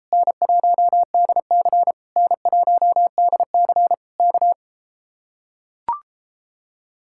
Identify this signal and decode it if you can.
Morse code, heard as a keyed tone, about 33 wpm, 700 Hz. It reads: N1DC N1DC K